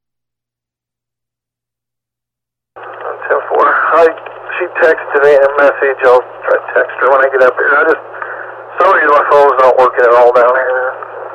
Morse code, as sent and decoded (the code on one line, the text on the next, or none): none